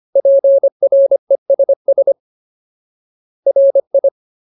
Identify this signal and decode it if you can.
Morse code, heard as a keyed tone, about 25 wpm, 550 Hz. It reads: PRESS RI
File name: Ultimate Word List / 25WPM Ultimate